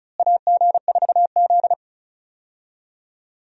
Morse code, heard as a keyed tone, about 36 wpm, 700 Hz. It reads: AG4Z
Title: USA Call Signs - 4 Characters - Repeated 2x - 35wpm